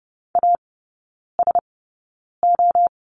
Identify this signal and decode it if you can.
Morse code, heard as a keyed tone, about 30 wpm, spaced as for 9 wpm, 700 Hz. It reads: ASO
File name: Lesson 19e - Letter w - Three Characters 30wpm